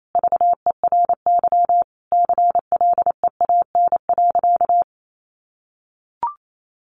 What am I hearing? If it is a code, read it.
Morse code, heard as a keyed tone, about 28 wpm, 700 Hz. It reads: VERY CLEAN.
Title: Sentences from Top 1100 Words 28wpm